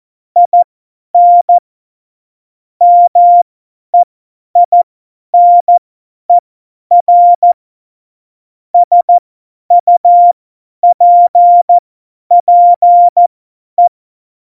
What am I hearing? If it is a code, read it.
Morse code, heard as a keyed tone, about 14 wpm, spaced as for 10 wpm, 700 Hz. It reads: IN MEINER SUPPE
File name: Morsekurs / Witze / Witz 17. September 2019, 50 F70